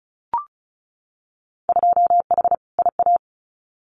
Morse code, heard as a keyed tone, about 34 wpm, 700 Hz. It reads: 2H IA
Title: Winter Field Day Exchanges Encore 35wpm